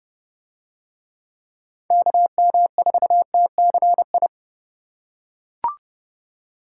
Morse code, heard as a keyed tone, about 30 wpm, 700 Hz. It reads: KM4TCI